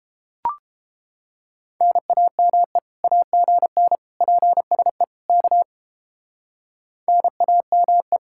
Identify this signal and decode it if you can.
Morse code, heard as a keyed tone, about 30 wpm, 700 Hz. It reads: NAME AGN PSE K NAME